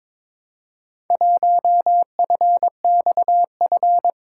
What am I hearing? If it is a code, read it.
Morse code, heard as a keyed tone, about 22 wpm, 700 Hz. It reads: 1FXF
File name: practice / 22wpm / Four Letter-Number Combinations - 22wpm